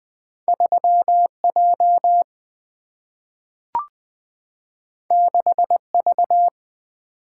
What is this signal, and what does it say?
Morse code, heard as a keyed tone, about 20 wpm, 700 Hz. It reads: 3J 6V
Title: Two Letter-Number Combinations - 20wpm